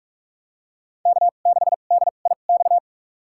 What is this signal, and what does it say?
Morse code, heard as a keyed tone, about 44 wpm, spaced as for 32 wpm, 690 Hz. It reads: K6DIX
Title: USA Call Signs - 5 Characters - 2x Character Spacing - 45wpm